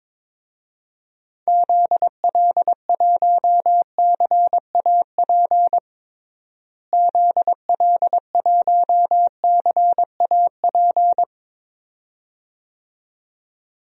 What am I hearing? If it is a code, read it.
Morse code, heard as a keyed tone, about 21 wpm, 700 Hz. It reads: ZL1CAP ZL1CAP
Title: DXCC-100 Call Signs - 6 Characters - Repeated 2x - 22wpm